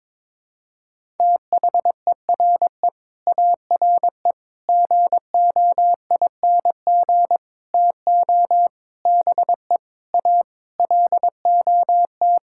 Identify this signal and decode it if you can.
Morse code, heard as a keyed tone, about 21 wpm, 700 Hz. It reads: THERE ARE GOING TO BE A LOT
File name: practice / 22wpm / Sets of 8 Words - 22wpm